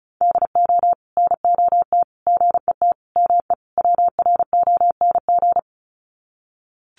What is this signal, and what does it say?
Morse code, heard as a keyed tone, about 34 wpm, 700 Hz. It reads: DO NOT GET ME WRONG
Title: Sets of 5 Words Encore 35wpm